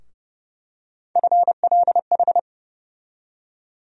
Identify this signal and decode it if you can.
Morse code, heard as a keyed tone, about 32 wpm, 700 Hz. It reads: FLH
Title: USA Call Sign Postfixes - 3 Letters 30wpm